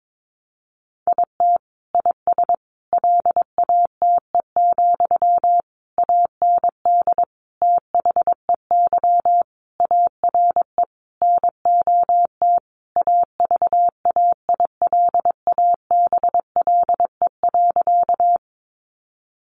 Morse code, heard as a keyed tone, about 22 wpm, 700 Hz. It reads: IT IS LATE, AND THEY ARE NOT AVAILABLE.
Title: Sentences from Top 600 Words 22wpm